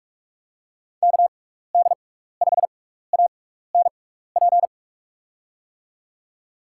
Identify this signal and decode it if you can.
Morse code, heard as a keyed tone, about 44 wpm, spaced as for 15 wpm, 700 Hz. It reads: KD5ANP
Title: USA Call Signs - 6 Characters - 6x Character Spacing - 45wpm